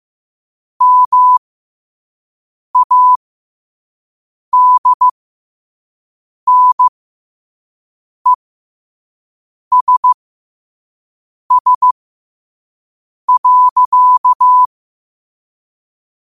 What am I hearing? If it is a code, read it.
Morse code, heard as a keyed tone, about 15 wpm, spaced as for 5 wpm, 1000 Hz. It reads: MADNESS.
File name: qotdfiles / 5wpm